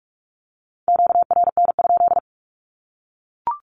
Morse code, heard as a keyed tone, about 44 wpm, 700 Hz. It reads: QRN?
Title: Q-Codes 45wpm